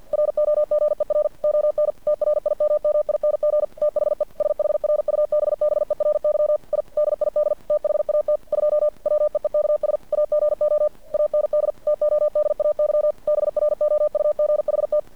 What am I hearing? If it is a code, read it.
Morse code, heard as a keyed tone, about 50 wpm, 600 Hz. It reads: ROGER ON TRIMMING THE SHRUBBERY I DID THAT 2 WEEKS AGO AND TODAY BROUGHT
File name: archived / CRSnet